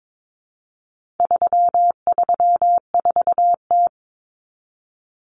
Morse code, heard as a keyed tone, about 22 wpm, 700 Hz. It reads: 334T
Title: Common Cut Numbers - 4-Digit 22wpm